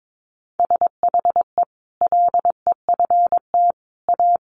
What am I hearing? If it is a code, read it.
Morse code, heard as a keyed tone, about 22 wpm, 700 Hz. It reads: SHE LEFT A
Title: Sentences from Top 1100 Words 22wpm